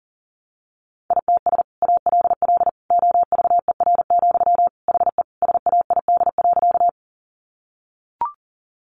Morse code, heard as a keyed tone, about 40 wpm, 700 Hz. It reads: ITS ALL OVER, HE SAID.
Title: Sentences from Top 600 Words 40wpm